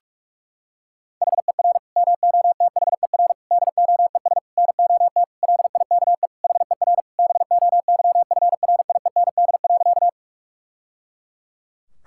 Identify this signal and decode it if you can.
Morse code, heard as a keyed tone, about 46 wpm, 700 Hz. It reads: HER MOTHER DOES NOT LIKE HER BOYFRIEND.
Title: Sentences from Top 1000 Words - Review Part 5 - 45wpm